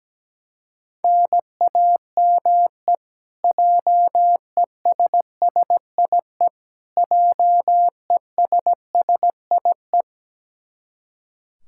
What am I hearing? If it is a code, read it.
Morse code, heard as a keyed tone, about 17 wpm, 700 Hz. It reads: NAME JESSIE JESSIE
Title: Practice QSO - Exchange 1 - Beginning - 17wpm